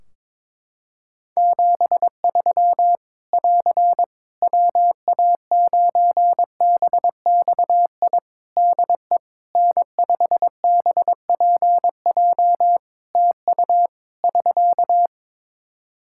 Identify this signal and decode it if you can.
Morse code, heard as a keyed tone, about 22 wpm, 700 Hz. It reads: 73 <AR> WA9BXI DE N5BPJ TU <SK>
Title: K1USN SST QSO Practice 22wpm